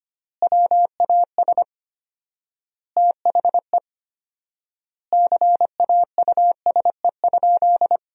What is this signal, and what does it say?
Morse code, heard as a keyed tone, about 25 wpm, 700 Hz. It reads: WAS THE CAUSE?
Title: 4x Word-Spacing - Sentences from Top 200 Words - 25wpm